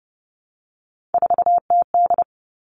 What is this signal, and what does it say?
Morse code, heard as a keyed tone, about 30 wpm, 700 Hz. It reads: 4TD
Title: All Cut Numbers - 3-Digit 30wpm